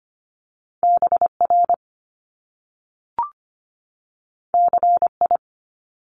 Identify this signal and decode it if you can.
Morse code, heard as a keyed tone, about 25 wpm, 700 Hz. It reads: BR CI
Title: Top Two Letter Combinations in Words 25wpm